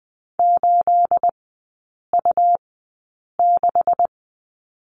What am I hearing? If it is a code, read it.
Morse code, heard as a keyed tone, about 20 wpm, spaced as for 8 wpm, 700 Hz. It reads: 8U6
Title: 218 Lesson 35e - Number 0 - Three Characters 20wpm